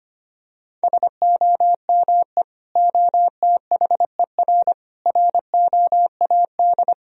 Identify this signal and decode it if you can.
Morse code, heard as a keyed tone, about 25 wpm, 700 Hz. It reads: SOME OTHER ROAD
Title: Sentences from Top 1000 Words - Review Part 4 - 25wpm